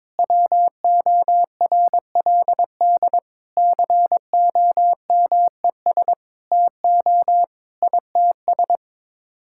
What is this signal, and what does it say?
Morse code, heard as a keyed tone, about 22 wpm, 700 Hz. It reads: WORLD COMES TO ITS
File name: Sets of 9 Words - 22wpm